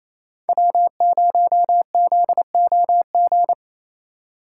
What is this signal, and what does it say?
Morse code, heard as a keyed tone, about 28 wpm, 700 Hz. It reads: W0ZOG
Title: USA Call Signs - 5 Characters - 2x Character Spacing - 28wpm